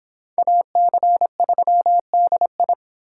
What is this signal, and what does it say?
Morse code, heard as a keyed tone, about 25 wpm, 700 Hz. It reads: AC3DI